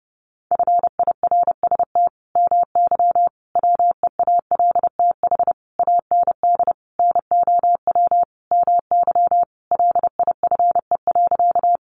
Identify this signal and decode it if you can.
Morse code, heard as a keyed tone, about 28 wpm, 700 Hz. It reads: FIRST MY WEALTH AND NOW MY LIFE.
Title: Sentences from Top 1200 Words 30wpm